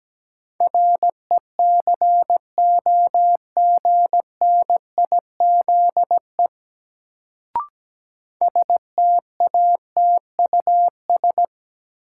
Morse code, heard as a keyed tone, about 17 wpm, 700 Hz. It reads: RECOGNIZE STATUS
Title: Sentences from Top 1400 Words - New Word Drill - 17wpm